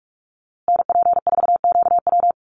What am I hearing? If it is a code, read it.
Morse code, heard as a keyed tone, about 44 wpm, 700 Hz. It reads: NP4QW
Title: USA Call Signs Repeated 3x 45wpm